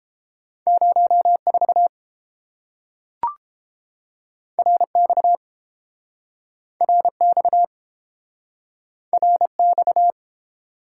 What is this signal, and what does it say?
Morse code, heard as a keyed tone, about 30 wpm, 700 Hz. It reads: 04 RX RX RX